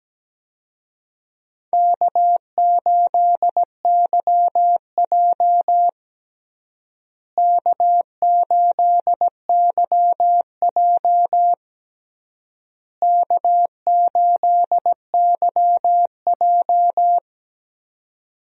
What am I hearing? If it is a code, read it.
Morse code, heard as a keyed tone, about 17 wpm, 700 Hz. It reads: K8YJ K8YJ K8YJ
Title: USA Call Signs - 4 Characters - Repeated 3x - 17wpm